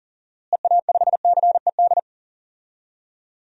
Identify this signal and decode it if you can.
Morse code, heard as a keyed tone, about 40 wpm, 700 Hz. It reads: EA5CED
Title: DXCC-100 Call Signs - 6 Characters - Repeated 3x - 40wpm